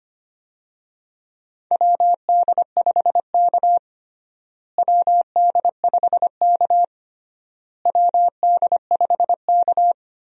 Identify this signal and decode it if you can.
Morse code, heard as a keyed tone, about 25 wpm, 700 Hz. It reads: WD5K WD5K WD5K
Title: USA Call Signs - 4 Characters - Repeated 3x - 25wpm